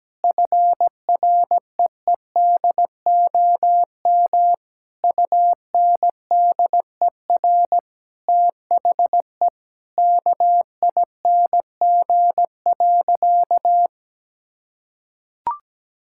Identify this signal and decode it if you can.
Morse code, heard as a keyed tone, about 17 wpm, 700 Hz. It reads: FREEDOM UNDER THE KING.
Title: Sentences from Top 1000 Words - Review Part 6 - 17wpm